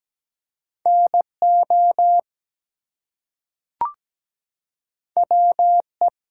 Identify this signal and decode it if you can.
Morse code, heard as a keyed tone, about 17 wpm, 700 Hz. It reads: NO WE
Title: Two Letter Words - 17wpm